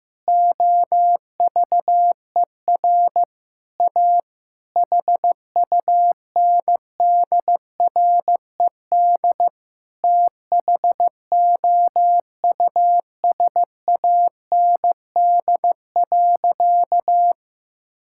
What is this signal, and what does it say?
Morse code, heard as a keyed tone, about 15 wpm, 700 Hz. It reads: OVER A HUNDRED THOUSAND.
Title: Sentences from Top 1000 Words - Review Part 5 - 15wpm